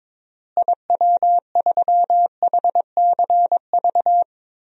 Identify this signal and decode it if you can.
Morse code, heard as a keyed tone, about 22 wpm, 700 Hz. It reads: IW3HCV
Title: DXCC-100 Call Signs - 6 Characters - Repeated 3x - 22wpm